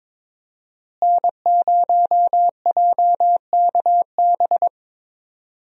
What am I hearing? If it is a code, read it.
Morse code, heard as a keyed tone, about 22 wpm, 700 Hz. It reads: N0JKB
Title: USA Call Signs - 5 Characters - Repeated 2x - 22wpm